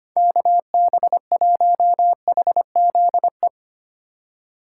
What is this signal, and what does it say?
Morse code, heard as a keyed tone, about 25 wpm, 690 Hz. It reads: KB1HZE
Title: USA Call Signs - 6 Characters - 2x Character Spacing - 25wpm